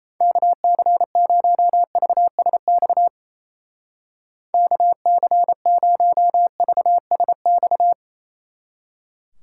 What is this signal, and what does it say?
Morse code, heard as a keyed tone, about 36 wpm, 700 Hz. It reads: KC0VSX KC0VSX